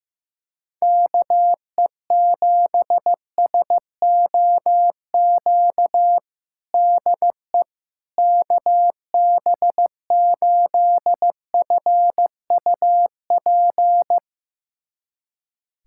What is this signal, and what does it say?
Morse code, heard as a keyed tone, about 16 wpm, 700 Hz. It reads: KE7SOQ DE KB8FUP
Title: USA Callsign DE Callsign 15wpm